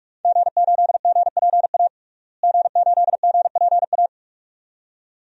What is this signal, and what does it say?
Morse code, heard as a keyed tone, about 41 wpm, 700 Hz. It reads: G8GPA G8GPA
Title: DXCC-100 Call Signs - 5 Characters - Repeated 2x - 45wpm